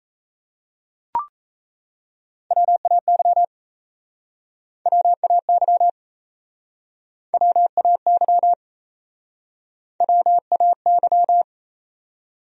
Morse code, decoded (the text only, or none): WAY WAY WAY WAY